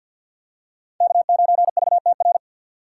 Morse code, heard as a keyed tone, about 49 wpm, 700 Hz. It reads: K9VTR
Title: USA Call Signs - 5 Characters - 3x Character Spacing - 50wpm